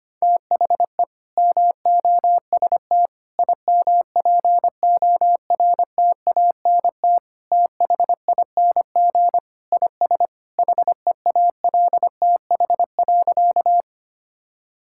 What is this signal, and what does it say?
Morse code, heard as a keyed tone, about 25 wpm, 700 Hz. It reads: THE MOST IMPORTANT THING IS HEALTH.
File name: practice / 25wpm / Sentences with 3-Word Phrases - 25wpm